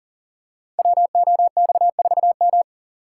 Morse code, heard as a keyed tone, about 42 wpm, 700 Hz. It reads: WOX4M
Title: Code Groups - 5 Letter:Numbers - 40wpm